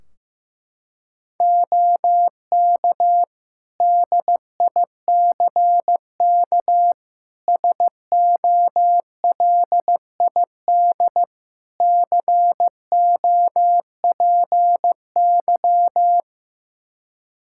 Morse code, heard as a keyed tone, about 15 wpm, 700 Hz. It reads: OK DICK SOLID COPY
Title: Practice QSO - Exchange 3 - More Info - 15wpm